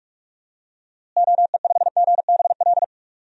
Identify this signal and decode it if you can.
Morse code, heard as a keyed tone, about 44 wpm, 700 Hz. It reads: OE5GBL
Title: DXCC-100 Call Signs - 6 Characters - Repeated 2x - 45wpm